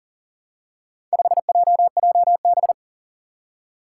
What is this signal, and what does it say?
Morse code, heard as a keyed tone, about 41 wpm, 700 Hz. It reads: 5JJB
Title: Four Challenging Letter-Number Combinations 40wpm